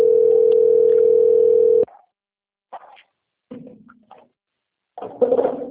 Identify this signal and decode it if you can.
Ringback tone (North America), 440 + 480 Hz.